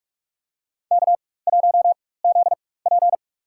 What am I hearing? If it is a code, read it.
Morse code, heard as a keyed tone, about 43 wpm, spaced as for 21 wpm, 700 Hz. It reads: K1ZP
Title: USA Call Signs - 4 Characters - 4x Character Spacing - 45wpm